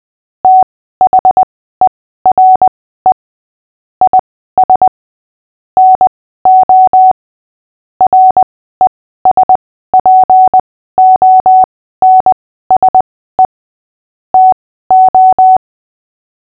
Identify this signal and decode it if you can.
Morse code, heard as a keyed tone, about 20 wpm, spaced as for 14 wpm, 750 Hz. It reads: THERE IS NO RESPONSE TO
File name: cw / fists / Short-Story-8-14wpm